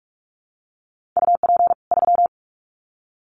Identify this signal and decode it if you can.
Morse code, heard as a keyed tone, about 44 wpm, 700 Hz. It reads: UP 3
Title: Common QSO Phrases 45wpm